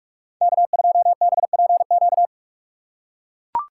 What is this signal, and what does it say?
Morse code, heard as a keyed tone, about 43 wpm, 700 Hz. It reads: K2BPQ
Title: USA Call Signs - 5 Characters - 6x Character Spacing - 45wpm